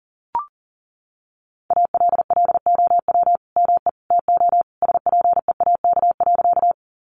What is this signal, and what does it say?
Morse code, heard as a keyed tone, about 40 wpm, 700 Hz. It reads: ALLOW ME TO SPEAK.
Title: Sentences from Top 1100 Words 40wpm